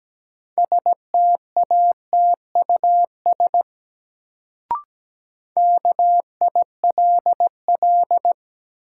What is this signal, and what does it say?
Morse code, heard as a keyed tone, about 17 wpm, 700 Hz. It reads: STATUS KILL
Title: Sentences from Top 1400 Words - New Word Drill - 17wpm